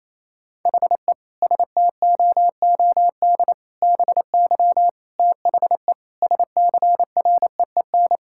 Morse code, heard as a keyed tone, about 28 wpm, 700 Hz. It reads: HE STOOD BY THE SCREEN
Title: Sentences from Top 1000 Words - Review Part 6 - 28wpm